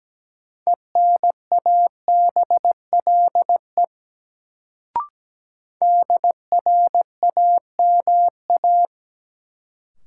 Morse code, heard as a keyed tone, about 17 wpm, 700 Hz. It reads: ENABLE DRAMA
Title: Sentences from Top 1500 Words - New Word Drill Alt - 17wpm